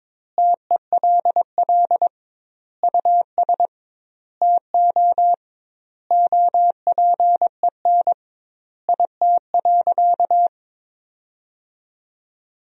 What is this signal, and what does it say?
Morse code, heard as a keyed tone, about 22 wpm, 700 Hz. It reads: TELL US TO OPEN IT.
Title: 2x Word-Spacing - Sentences from Top 300 Words - 22wpm